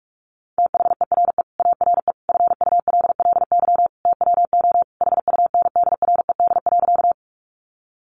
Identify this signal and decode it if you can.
Morse code, heard as a keyed tone, about 44 wpm, 700 Hz. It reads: THERE ARE FULLY TWO HUNDRED.
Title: Sentences from Top 1300 Words 45wpm